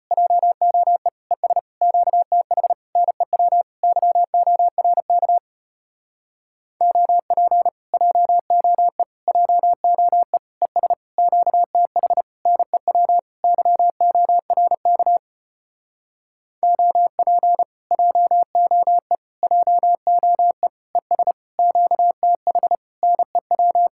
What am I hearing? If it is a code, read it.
Morse code, heard as a keyed tone, about 33 wpm, 700 Hz. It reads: JOE ES QTH NEW YORK OP JOE JOE ES QTH NEW YORK OP JOE JOE ES QTH NEW